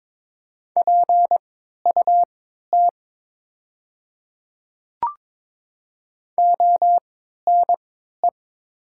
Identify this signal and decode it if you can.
Morse code, heard as a keyed tone, about 22 wpm, spaced as for 13 wpm, 700 Hz. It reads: PUT ONE
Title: 3x Character Spacing - Top 200 Words - 22wpm